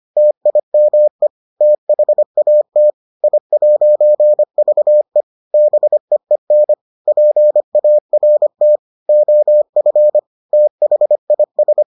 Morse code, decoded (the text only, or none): TIME THAT I'VE BEEN PART OF THIS